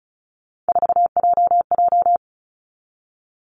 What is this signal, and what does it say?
Morse code, heard as a keyed tone, about 36 wpm, 700 Hz. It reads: 4JJ
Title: Three Challenging Letter-Number Combinations 35wpm